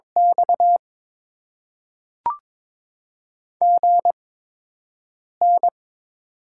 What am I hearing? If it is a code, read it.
Morse code, heard as a keyed tone, about 22 wpm, spaced as for 6 wpm, 700 Hz. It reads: X GN
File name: Top 90 QSO Elements - 8x Character Spacing - 22wpm